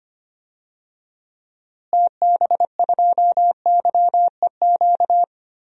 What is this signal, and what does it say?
Morse code, heard as a keyed tone, about 25 wpm, 700 Hz. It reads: TB2YEQ
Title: DXCC-100 Call Signs - 6 Characters - Repeated 2x - 25wpm